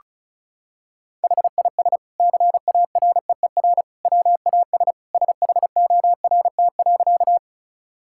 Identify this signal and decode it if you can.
Morse code, heard as a keyed tone, about 34 wpm, 700 Hz. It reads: HIS CAREER WAS SHORT.
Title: Sentences from Top 1000 Words - Review Part 6 - 35wpm